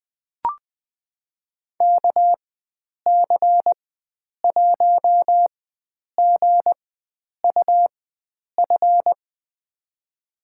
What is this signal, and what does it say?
Morse code, heard as a keyed tone, about 20 wpm, spaced as for 9 wpm, 700 Hz. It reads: KC1GUF
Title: USA Call Signs - 6 Characters - 4x Character Spacing - 20wpm